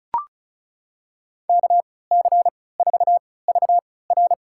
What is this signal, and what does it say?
Morse code, heard as a keyed tone, about 34 wpm, spaced as for 20 wpm, 700 Hz. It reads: KC4VR
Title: USA Call Signs - 5 Characters - 3x Character Spacing - 35wpm